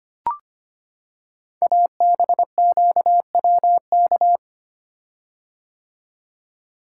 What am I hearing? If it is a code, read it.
Morse code, heard as a keyed tone, about 25 wpm, 700 Hz. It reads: ABQWK
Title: Code Groups - 5 Letters - 25wpm